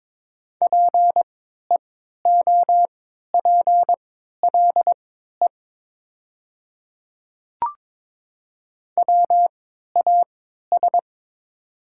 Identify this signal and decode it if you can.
Morse code, heard as a keyed tone, about 22 wpm, spaced as for 13 wpm, 700 Hz. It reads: PEOPLE WAS
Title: 3x Character Spacing - Top 200 Words - 22wpm